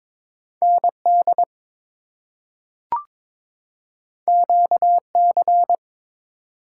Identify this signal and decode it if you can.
Morse code, heard as a keyed tone, about 21 wpm, 700 Hz. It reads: ND QC
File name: ARRL Sections v2 - 22wpm